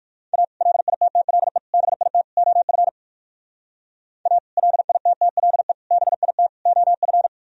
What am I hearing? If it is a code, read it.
Morse code, heard as a keyed tone, about 49 wpm, 700 Hz. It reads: A LITTLE BIT OF A LITTLE BIT OF
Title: Speed-Racing - Sets of 4 Words 45wpm